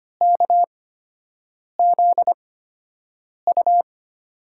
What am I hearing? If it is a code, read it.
Morse code, heard as a keyed tone, about 25 wpm, spaced as for 7 wpm, 700 Hz. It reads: KZU
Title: USA Call Signs - 5 Characters - 8x Character Spacing - 25wpm